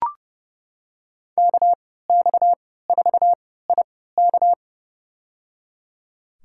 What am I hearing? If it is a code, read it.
Morse code, heard as a keyed tone, about 30 wpm, spaced as for 17 wpm, 700 Hz. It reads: KX4IK
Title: USA Call Signs - 5 Characters - 3x Character Spacing - 30wpm